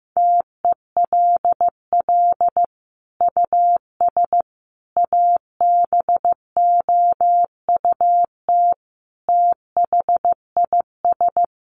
Morse code, heard as a keyed tone, about 15 wpm, 700 Hz. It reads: TELL US ABOUT THIS
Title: Sentences from Top 1200 Words 15wpm